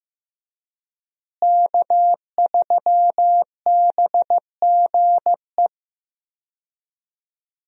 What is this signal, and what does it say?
Morse code, heard as a keyed tone, about 15 wpm, 700 Hz. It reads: K3BGE
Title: USA Call Signs - 5 Characters - Repeated 2x - 15wpm